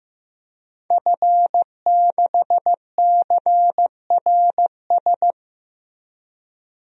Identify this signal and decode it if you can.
Morse code, heard as a keyed tone, about 15 wpm, 700 Hz. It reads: F6CRS
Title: DXCC-100 Call Signs - 5 Characters - 15wpm